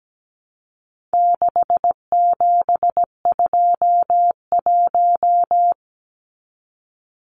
Morse code, heard as a keyed tone, about 17 wpm, 700 Hz. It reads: 6721
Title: Numbers - 4-Digit 17wpm